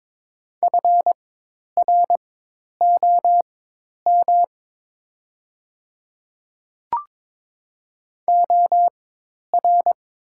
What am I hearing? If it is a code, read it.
Morse code, heard as a keyed tone, about 22 wpm, spaced as for 10 wpm, 700 Hz. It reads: FROM OR
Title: 4x Character Spacing - Top 200 Words - 22wpm